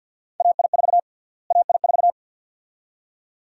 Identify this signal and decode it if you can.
Morse code, heard as a keyed tone, about 46 wpm, 700 Hz. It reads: AI4 AI4
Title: USA Call Sign Prefixes - Repeated 2x - 50wpm